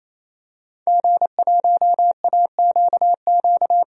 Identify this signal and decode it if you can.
Morse code, heard as a keyed tone, about 28 wpm, 700 Hz. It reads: G1AQQ
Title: DXCC-100 Call Signs - 5 Characters - 28wpm